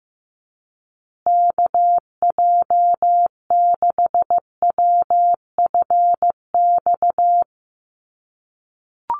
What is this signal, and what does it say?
Morse code, heard as a keyed tone, about 15 wpm, 700 Hz. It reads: KJ6WFX
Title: USA Call Signs Repeated 3x 15wpm